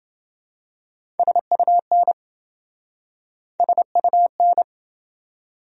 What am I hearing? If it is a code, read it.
Morse code, heard as a keyed tone, about 27 wpm, 700 Hz. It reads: SUN SUN